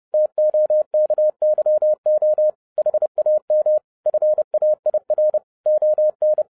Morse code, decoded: TOKYO HAM FAIR ON

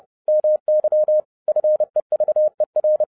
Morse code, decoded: MY FEVER

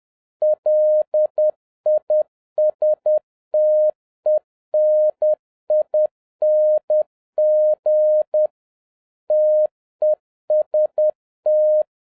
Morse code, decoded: LISTENING TEST

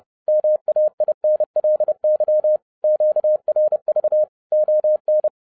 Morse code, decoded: MAINLY QRV ON

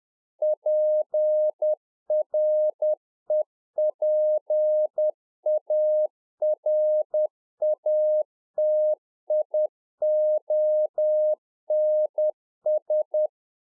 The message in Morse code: PREPARATIONS

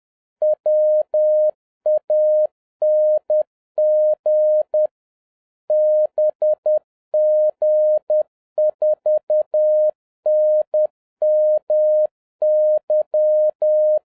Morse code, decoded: WANG BG4NMY

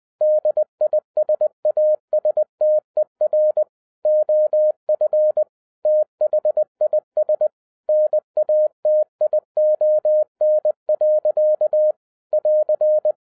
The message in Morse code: DISASTER OF THIS NATION. <AR>

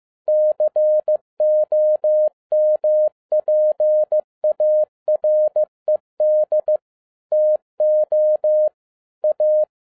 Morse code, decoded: COMPARED TO A

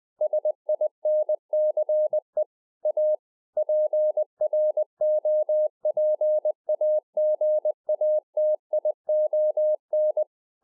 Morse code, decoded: SINCE A PROPAGATION

